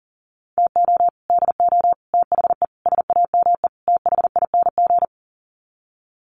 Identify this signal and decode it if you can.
Morse code, heard as a keyed tone, about 40 wpm, 700 Hz. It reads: TO DO THE SAME THING
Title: Sets of 5 Words Encore 40wpm